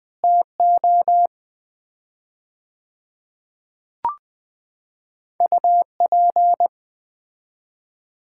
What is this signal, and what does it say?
Morse code, heard as a keyed tone, about 20 wpm, 700 Hz. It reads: TO UP
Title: 4x Word-Spacing - Sets of 2 Words - 20wpm